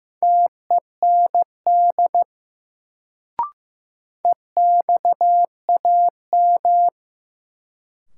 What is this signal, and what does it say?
Morse code, heard as a keyed tone, about 15 wpm, 700 Hz. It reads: TEND EXAM
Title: Sentences from Top 1000 Words - Review Part 5 - New Word Drill - 15wpm